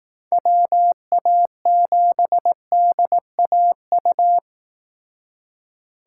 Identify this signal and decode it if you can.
Morse code, heard as a keyed tone, about 18 wpm, 700 Hz. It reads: WA7DAU
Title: Speed Racing - USA 2x3 Call Signs 15wpm